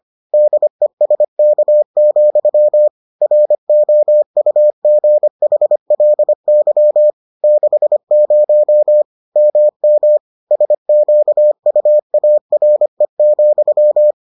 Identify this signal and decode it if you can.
Morse code, heard as a keyed tone, about 25 wpm, 600 Hz. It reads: DESK, ROUGHLY 60 MM SQUARE,